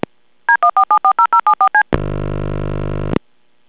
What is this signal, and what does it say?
DTMF, heard as telephone touch tones, tones of about 90 ms. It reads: D14740074C
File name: callerid before ring